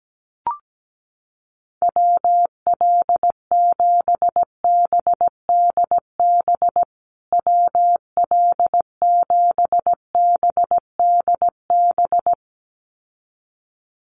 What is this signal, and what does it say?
Morse code, heard as a keyed tone, about 17 wpm, 700 Hz. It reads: WL7BDB WL7BDB
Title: USA Call Signs Repeated 2x 17wpm